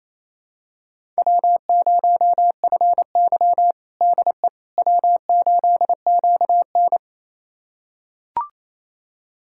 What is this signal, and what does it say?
Morse code, heard as a keyed tone, about 28 wpm, 700 Hz. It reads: W0FY DE W8QN
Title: USA Callsign DE Callsign - 4 Characters - 28wpm